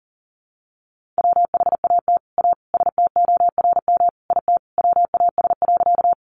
Morse code, heard as a keyed tone, about 40 wpm, 700 Hz. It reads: WHAT A STORM IT WAS.
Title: Sentences from Top 1500 Words 40wpm